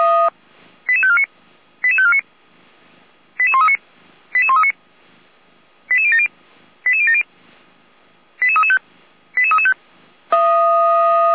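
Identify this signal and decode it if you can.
Five-tone selective call (ZVEI), heard as quick runs of five tones, tones of about 70 ms. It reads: 88549 88549 88139 88139 88070 88070 88335 88335
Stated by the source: ZVEI five-tone selective call 88409 received on an SDR.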